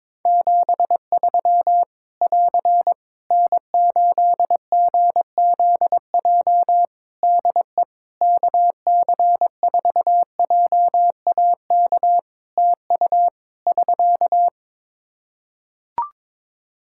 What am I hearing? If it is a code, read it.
Morse code, heard as a keyed tone, about 22 wpm, 700 Hz. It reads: 73 <AR> N8GZJ DE KC4JAK TU <SK>
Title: Practice QSO - Exchange 4 - Ending - 22wpm